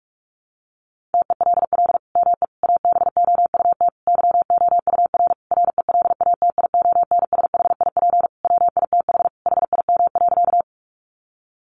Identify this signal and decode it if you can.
Morse code, heard as a keyed tone, about 44 wpm, 700 Hz. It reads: TELL ME ABOUT YOUR RELATIONSHIP WITH HIM.
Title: Sentences from Top 1100 Words 45wpm